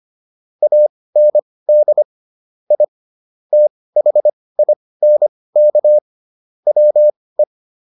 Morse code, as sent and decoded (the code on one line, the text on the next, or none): .- -. -.. / .. / - .... .. -. -.- / .-- .
AND I THINK WE